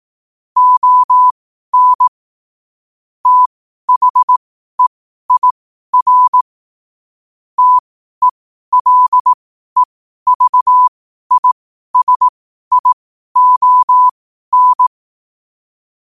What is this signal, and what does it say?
Morse code, heard as a keyed tone, about 18 wpm, spaced as for 12 wpm, 1000 Hz. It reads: ON THEIR TELEVISION
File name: qotdfiles / 12wpm